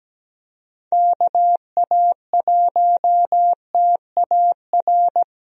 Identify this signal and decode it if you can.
Morse code, heard as a keyed tone, about 17 wpm, 700 Hz. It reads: KA1TAR
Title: USA Call Signs - 6 Characters - Repeated 2x - 17wpm